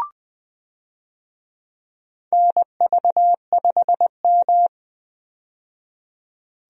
Morse code, heard as a keyed tone, about 20 wpm, 700 Hz. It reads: NV5M
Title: USA Call Signs - 4 Characters - 20wpm